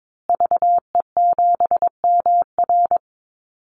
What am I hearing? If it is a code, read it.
Morse code, heard as a keyed tone, about 22 wpm, 700 Hz. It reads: VE7MR
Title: Call Signs Repeated 3x - 22wpm